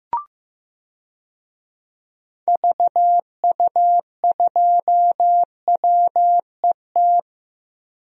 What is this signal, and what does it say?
Morse code, heard as a keyed tone, about 15 wpm, 700 Hz. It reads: VU2WET
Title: DXCC-100 Call Signs - 6 Characters - 15wpm